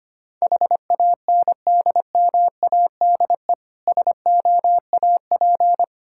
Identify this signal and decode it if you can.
Morse code, heard as a keyed tone, about 25 wpm, 700 Hz. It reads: HANDMADE SOAP
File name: practice / 25wpm / Difficult Words with Hints - 25wpm